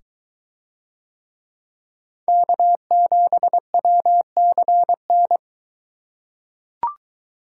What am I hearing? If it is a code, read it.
Morse code, heard as a keyed tone, about 23 wpm, 700 Hz. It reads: K7WCN